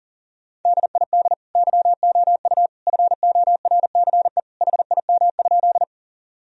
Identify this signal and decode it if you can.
Morse code, heard as a keyed tone, about 40 wpm, 700 Hz. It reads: DID YOU FORCE HIM?
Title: Sentences from Top 1000 Words - Review Part 3 - 40wpm